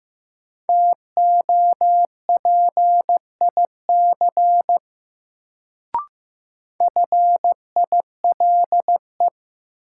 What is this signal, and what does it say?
Morse code, heard as a keyed tone, about 16 wpm, 700 Hz. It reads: TOPIC FILE
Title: Sentences from Top 1200 Words - New Word Drill Alt - 15wpm